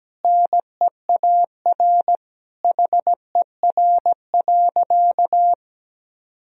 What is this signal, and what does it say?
Morse code, heard as a keyed tone, about 17 wpm, 700 Hz. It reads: NEAR HER.
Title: Sentences from Top 1000 Words - Review Part 2 - 17wpm